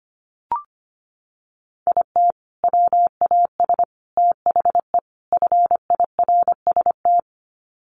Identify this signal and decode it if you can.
Morse code, heard as a keyed tone, about 25 wpm, 700 Hz. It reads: IT WAS THE FIRST
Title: Sets of 4 Words 25wpm